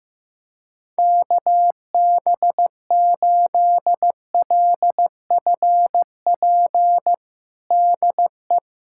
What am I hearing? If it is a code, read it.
Morse code, heard as a keyed tone, about 15 wpm, 700 Hz. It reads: KB8LFP DE
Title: USA Callsign DE Callsign - 6 Characters - 15wpm